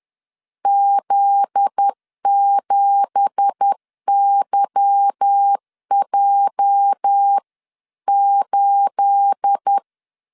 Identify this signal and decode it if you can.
Morse code, heard as a keyed tone, about 10 wpm, 800 Hz. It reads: Z7YJ8
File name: CW / DET / 1 9 1500C S A 10wpm 4511